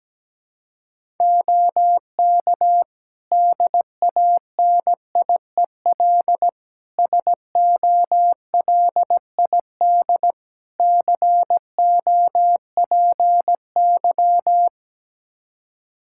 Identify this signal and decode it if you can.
Morse code, heard as a keyed tone, about 17 wpm, 700 Hz. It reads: OK DANIEL SOLID COPY
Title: K1USN SST QSO Practice 17wpm